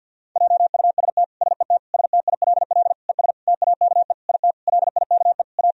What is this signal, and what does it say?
Morse code, heard as a keyed tone, about 50 wpm, 700 Hz. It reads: JUST SET STILL ES TAKE IT LIKE A
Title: Sentences from Top 300 Words and Abbreviations 50wpm